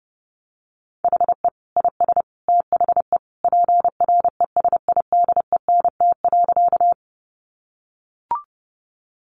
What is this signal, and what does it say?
Morse code, heard as a keyed tone, about 30 wpm, 700 Hz. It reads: HE IS THE PRESIDENT.
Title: Sentences from Top 1000 Words 30wpm